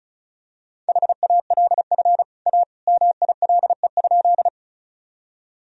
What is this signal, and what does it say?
Morse code, heard as a keyed tone, about 36 wpm, 700 Hz. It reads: HALF A MILE?
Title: Sentences from Top 1000 Words - Review Part 1 - 35wpm